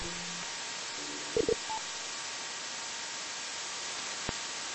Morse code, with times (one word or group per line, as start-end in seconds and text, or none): none